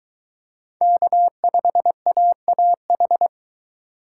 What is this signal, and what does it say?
Morse code, heard as a keyed tone, about 22 wpm, 700 Hz. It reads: K5AAH